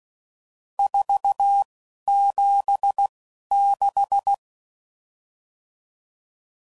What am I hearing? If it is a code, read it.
Morse code, heard as a keyed tone, about 16 wpm, spaced as for 11 wpm, 800 Hz. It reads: 476
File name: CW Code Groups @10WPM/RSTs/Various RST No's